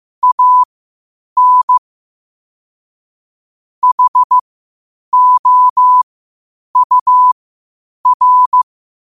Morse code, heard as a keyed tone, about 15 wpm, spaced as for 8 wpm, 1000 Hz. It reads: AN HOUR
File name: qotdfiles / 8wpm